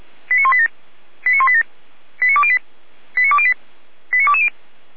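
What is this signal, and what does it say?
Five-tone selective call (ZVEI), heard as quick runs of five tones, tones of about 70 ms. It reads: 89187 89187 89208 89208 89220
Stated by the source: ZVEI five-tone call received on SDR by the Orlamünde fire brigade (feuerwehr-orlamuende).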